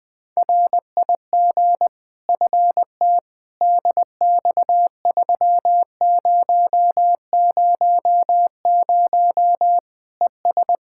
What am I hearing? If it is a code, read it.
Morse code, heard as a keyed tone, about 20 wpm, 700 Hz. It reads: RIG FT DX3000 ES
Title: Practice QSO - Exchange 2 - Info - 20wpm